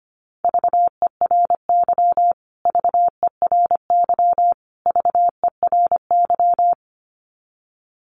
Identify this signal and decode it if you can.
Morse code, heard as a keyed tone, about 25 wpm, 700 Hz. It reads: VERY VERY VERY
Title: Top 500 Words Repeated 25wpm